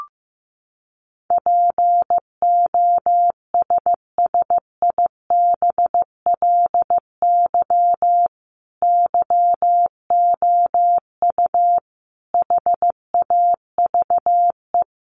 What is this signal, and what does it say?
Morse code, heard as a keyed tone, about 15 wpm, 700 Hz. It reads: POSSIBLY YOU HAVE
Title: Sentences from Top 1300 Words 15wpm